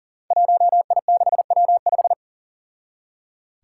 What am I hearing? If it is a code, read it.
Morse code, heard as a keyed tone, about 39 wpm, 700 Hz. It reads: 1I6W5
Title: Five Challenging Letter-Number Combinations 40wpm